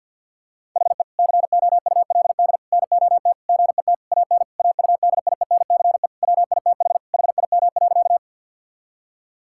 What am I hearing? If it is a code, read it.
Morse code, heard as a keyed tone, about 50 wpm, 700 Hz. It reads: HE COULD NOT GET AN AUDIENCE WITH HIM.
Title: Sentences from Top 1000 Words - Review Part 5 - 50wpm